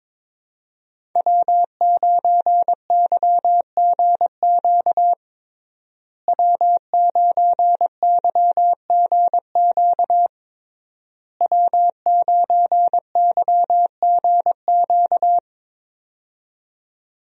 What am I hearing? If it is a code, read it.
Morse code, heard as a keyed tone, about 22 wpm, 700 Hz. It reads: W9YGQ W9YGQ W9YGQ
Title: USA Call Signs - 5 Characters - Repeated 3x - 22wpm